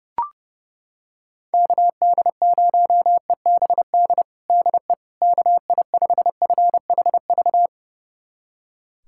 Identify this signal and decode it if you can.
Morse code, heard as a keyed tone, about 30 wpm, 700 Hz. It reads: KD0EBD DE KI5FHV
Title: USA Callsign DE Callsign - 6 Characters - 30wpm